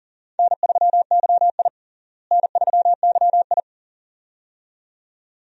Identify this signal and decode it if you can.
Morse code, heard as a keyed tone, about 40 wpm, 700 Hz. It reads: N3YI N3YI
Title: USA Call Signs - 4 Characters - Repeated 2x - 40wpm